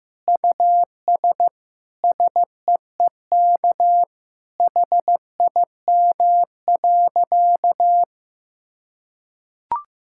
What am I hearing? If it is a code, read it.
Morse code, heard as a keyed tone, about 15 wpm, 700 Hz. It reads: US SEEK HIM.
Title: Sentences from Top 1600 Words 15wpm